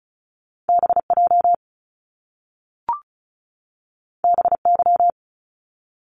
Morse code, heard as a keyed tone, about 36 wpm, 700 Hz. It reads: BJ BY
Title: Top Two Letter Combinations in Words 35wpm